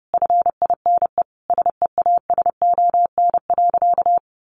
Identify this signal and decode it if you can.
Morse code, heard as a keyed tone, about 30 wpm, 700 Hz. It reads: FINE SEASON.
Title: Sentences from Top 1400 Words 30wpm